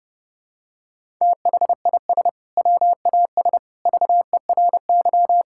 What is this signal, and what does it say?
Morse code, heard as a keyed tone, about 30 wpm, 700 Hz. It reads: THIS WAS VERY
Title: Sentences from Top 1000 Words - Review Part 6 - 30wpm